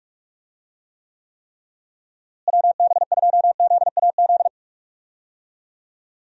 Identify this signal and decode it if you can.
Morse code, heard as a keyed tone, about 46 wpm, 700 Hz. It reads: WB2ZAZ
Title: USA Call Signs - 6 Characters - Repeated 3x - 45wpm